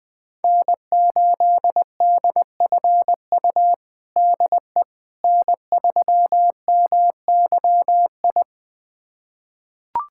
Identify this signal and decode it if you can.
Morse code, heard as a keyed tone, about 20 wpm, 700 Hz. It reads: N8DFU DE N3MYI
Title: USA Callsign DE Callsign - 5 Characters - 20wpm